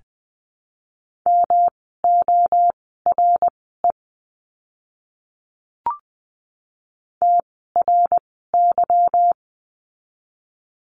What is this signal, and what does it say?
Morse code, heard as a keyed tone, about 20 wpm, spaced as for 15 wpm, 700 Hz. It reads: MORE TRY
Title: 160 Lesson 26f - Letter g - Words 20wpm